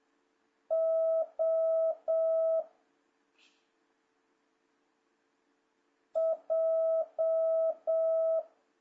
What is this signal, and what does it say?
Morse code, heard as a keyed tone, about 7 wpm, 650 Hz. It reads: O J
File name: jh8cbh / 2026cw